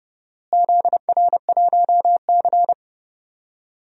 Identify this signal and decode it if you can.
Morse code, heard as a keyed tone, about 30 wpm, 700 Hz. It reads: ZR1C
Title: DXCC-100 Call Signs - 4 Characters - Repeated 3x - 30wpm